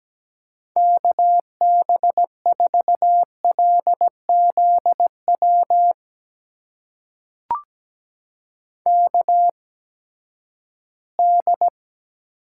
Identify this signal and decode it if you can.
Morse code, heard as a keyed tone, about 17 wpm, spaced as for 12 wpm, 700 Hz. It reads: KB4LZW KD
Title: USA Call Signs - 6 Characters - 8x Character Spacing - 17wpm